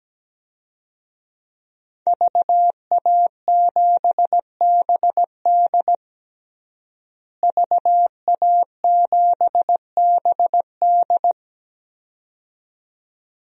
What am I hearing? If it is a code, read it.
Morse code, heard as a keyed tone, about 17 wpm, 700 Hz. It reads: VA7BD VA7BD
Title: DXCC-100 Call Signs - 5 Characters - Repeated 2x - 17wpm